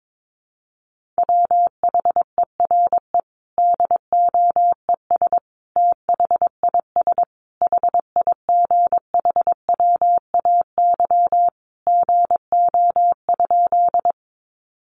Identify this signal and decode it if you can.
Morse code, heard as a keyed tone, about 22 wpm, 700 Hz. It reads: WHERE DOES THIS HIGHWAY GO?
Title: Sentences from Top 1400 Words 22wpm